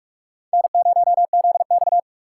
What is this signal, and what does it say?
Morse code, heard as a keyed tone, about 46 wpm, 700 Hz. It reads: N0ZX
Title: USA Call Signs - 4 Characters - 3x Character Spacing - 45wpm